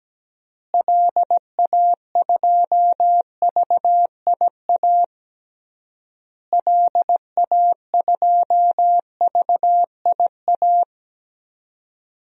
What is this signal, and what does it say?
Morse code, heard as a keyed tone, about 17 wpm, 700 Hz. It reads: LA2VIA LA2VIA
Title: DXCC-100 Call Signs - 6 Characters - Repeated 3x - 17wpm